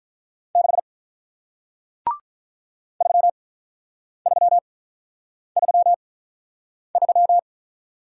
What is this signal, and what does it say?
Morse code, heard as a keyed tone, about 47 wpm, spaced as for 9 wpm, 700 Hz. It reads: 6 3333